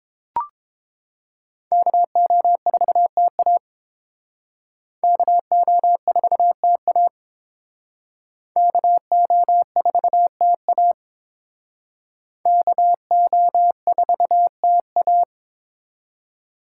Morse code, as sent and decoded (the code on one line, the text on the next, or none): -.- --- ....- - .- / -.- --- ....- - .- / -.- --- ....- - .- / -.- --- ....- - .-
KO4TA KO4TA KO4TA KO4TA